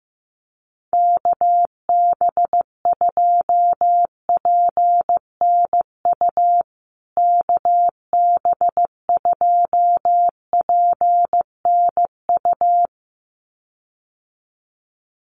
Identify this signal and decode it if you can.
Morse code, heard as a keyed tone, about 15 wpm, 700 Hz. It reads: KB2PNU KB2PNU
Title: USA Call Signs Repeated 2x 15wpm